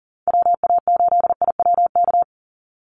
Morse code, heard as a keyed tone, about 40 wpm, 700 Hz. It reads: WA8IWK
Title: Call Signs - 40wpm